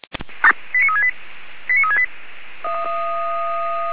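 Five-tone selective call (ZVEI), heard as quick runs of five tones, tones of about 80 ms. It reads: 89368 89368
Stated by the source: Five-tone ZVEI selective call received by SDR, address 89300.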